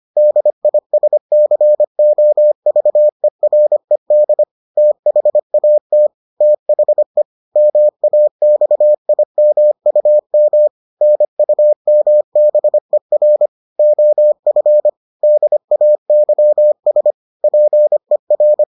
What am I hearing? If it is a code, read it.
Morse code, heard as a keyed tone, about 25 wpm, 600 Hz. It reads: DISCOVERED THAT THE MAXIMUM NUMBER OF DAYS PER